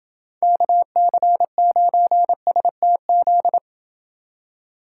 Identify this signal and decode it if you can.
Morse code, heard as a keyed tone, about 30 wpm, 700 Hz. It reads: KC9STZ